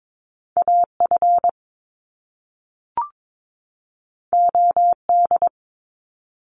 Morse code, heard as a keyed tone, about 22 wpm, 700 Hz. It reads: AF OD
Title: Top Two Letter Combinations in Words 22wpm